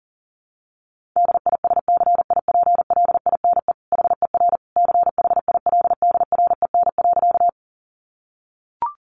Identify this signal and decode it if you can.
Morse code, heard as a keyed tone, about 40 wpm, 700 Hz. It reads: DISCIPLINE HER CHILDREN.
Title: Sentences from Top 900 Words 40wpm